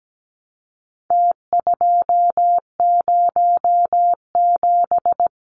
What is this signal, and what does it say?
Morse code, heard as a keyed tone, about 17 wpm, 700 Hz. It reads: T207
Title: Numbers - Contest Style - 4-Digit 17wpm